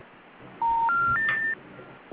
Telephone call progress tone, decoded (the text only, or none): special information tone